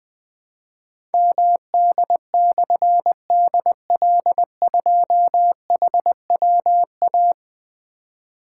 Morse code, heard as a keyed tone, about 20 wpm, 700 Hz. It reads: MD/DL2HWA
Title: Difficult Call Signs 20wpm